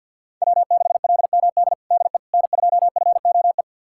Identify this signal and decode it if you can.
Morse code, heard as a keyed tone, about 46 wpm, 700 Hz. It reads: W6LMD DE N2FOE
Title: USA Callsign DE Callsign - 5 Characters - 50wpm